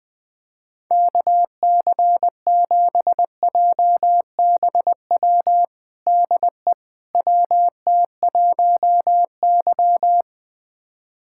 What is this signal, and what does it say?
Morse code, heard as a keyed tone, about 20 wpm, 700 Hz. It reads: KC7JBW DE WT1Y
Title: USA Callsign DE Callsign 20wpm